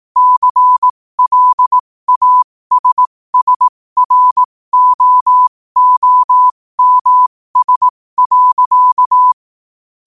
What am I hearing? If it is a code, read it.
Morse code, heard as a keyed tone, about 18 wpm, spaced as for 15 wpm, 1000 Hz. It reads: CLASSROOMS.
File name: qotdfiles / 15wpm